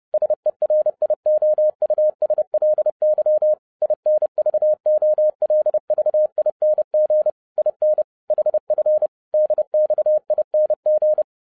Morse code, heard as a keyed tone, about 30 wpm, 610 Hz. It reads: SERIOUSLY INVOLVING IN HF DXING
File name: 2012-01-mp3 / n11-I DIDNT MISS THE SOLAR CYCLE 24-30wpm